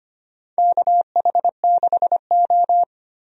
Morse code, heard as a keyed tone, about 25 wpm, 700 Hz. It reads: KH6O
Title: DXCC-100 Call Signs - 4 Characters - 25wpm